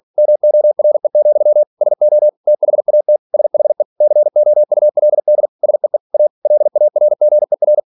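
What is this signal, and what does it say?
Morse code, heard as a keyed tone, about 46 wpm, 600 Hz. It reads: MORE, SO THAT SHE COULD SEE A LARGER
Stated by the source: Practice text from an English novel sent at 47 WPM.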